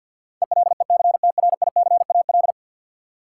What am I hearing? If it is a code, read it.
Morse code, heard as a keyed tone, about 53 wpm, 700 Hz. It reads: ELECTRICAL
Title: Sentences from Top 1000 Words - Review Part 6 - New Word Drill - 50wpm